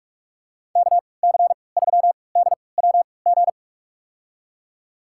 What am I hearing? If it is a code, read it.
Morse code, heard as a keyed tone, about 44 wpm, spaced as for 25 wpm, 700 Hz. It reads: KC3DWG
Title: USA Call Signs - 6 Characters - 3x Character Spacing - 45wpm